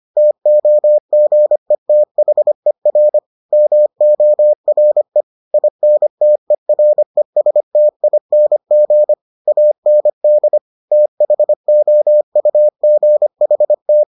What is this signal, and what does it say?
Morse code, heard as a keyed tone, about 25 wpm, 600 Hz. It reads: TOGETHER MORE INTERESTING AND THOUGHT